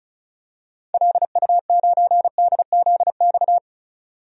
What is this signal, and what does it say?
Morse code, heard as a keyed tone, about 36 wpm, 700 Hz. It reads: LU9DZX
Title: DXCC-100 Call Signs - 6 Characters - 35wpm